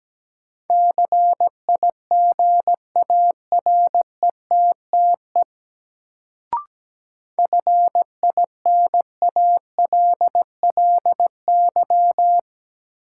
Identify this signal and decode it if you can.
Morse code, heard as a keyed tone, about 17 wpm, 700 Hz. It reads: CIGARETTE FINALLY
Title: Sentences from Top 1000 Words - New Word Drill - 17wpm